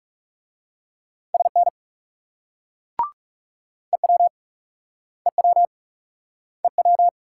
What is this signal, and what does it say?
Morse code, heard as a keyed tone, about 38 wpm, 700 Hz. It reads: SN EW EW EW